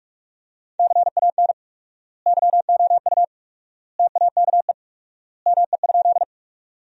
Morse code, heard as a keyed tone, about 45 wpm, 700 Hz. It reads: CAN YOU TAKE ME?
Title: 4x Word-Spacing - Sentences from Top 200 Words - 45wpm